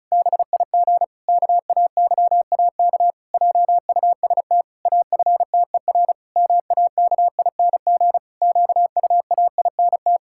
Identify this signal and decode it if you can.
Morse code, heard as a keyed tone, about 36 wpm, 700 Hz. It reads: BIG KAYAK JUST AFTER MAKING QUAINT
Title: Pangram Challenge v2 - 35wpm